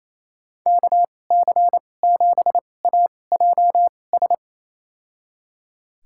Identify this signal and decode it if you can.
Morse code, heard as a keyed tone, about 28 wpm, spaced as for 20 wpm, 700 Hz. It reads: KC7AJS
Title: USA Call Signs - 6 Characters - 2x Character Spacing - 28wpm